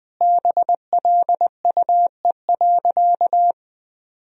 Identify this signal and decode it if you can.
Morse code, heard as a keyed tone, about 20 wpm, 700 Hz. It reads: BLUE.
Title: Sentences from Top 1000 Words - Review Part 3 - 20wpm